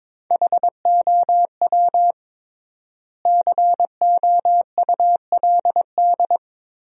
Morse code, heard as a keyed tone, about 22 wpm, 700 Hz. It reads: HOW COULD
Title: 3x Word-Spacing - Sentences from Top 400 Words - 22wpm